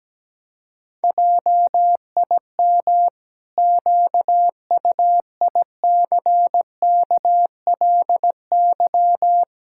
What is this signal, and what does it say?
Morse code, heard as a keyed tone, about 17 wpm, 700 Hz. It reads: JIM QUICKLY
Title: Pangram Challenge v2 - 17wpm